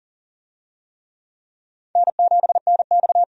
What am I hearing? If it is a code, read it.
Morse code, heard as a keyed tone, about 40 wpm, 700 Hz. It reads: N7NX
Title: USA Call Signs - 4 Characters - 40wpm